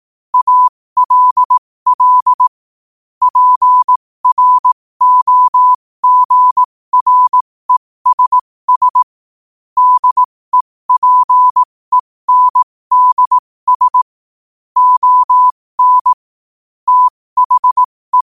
Morse code, heard as a keyed tone, about 18 wpm, spaced as for 15 wpm, 1000 Hz. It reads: ALL PROGRESS DEPENDS ON THE
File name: qotdfiles / 15wpm